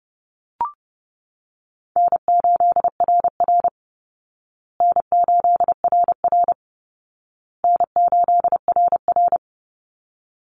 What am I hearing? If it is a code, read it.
Morse code, heard as a keyed tone, about 30 wpm, 700 Hz. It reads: N8RR N8RR N8RR
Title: Call Signs Repeated 2x - 30wpm